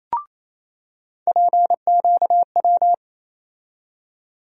Morse code, heard as a keyed tone, about 27 wpm, 700 Hz. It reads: PQW
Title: USA Call Sign Postfixes - 3 Letters 28wpm